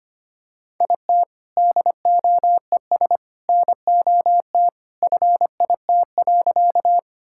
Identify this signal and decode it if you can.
Morse code, heard as a keyed tone, about 25 wpm, 700 Hz. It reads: IT DOES NOT FIT.
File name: practice / 25wpm / Sentences with 2-Word Phrases - 25wpm